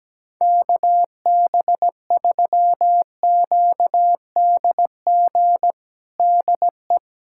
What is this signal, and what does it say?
Morse code, heard as a keyed tone, about 17 wpm, 700 Hz. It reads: KB3QDG DE
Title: USA Callsign DE Callsign - 6 Characters - 17wpm